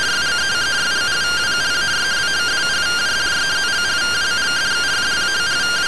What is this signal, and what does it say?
Two-tone teleprinter signal, about 170 Hz shift.